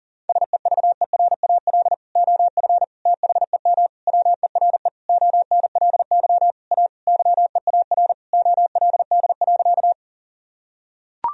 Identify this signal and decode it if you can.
Morse code, heard as a keyed tone, about 40 wpm, 700 Hz. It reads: SEVERAL OF THEM WERE ONLY A YEAR OLD.
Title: Sentences from Top 1000 Words - Review Part 2 - 40wpm